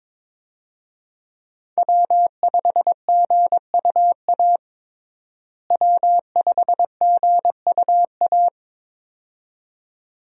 Morse code, heard as a keyed tone, about 22 wpm, 700 Hz. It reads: W5GUA W5GUA
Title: USA Call Signs - 5 Characters - Repeated 2x - 22wpm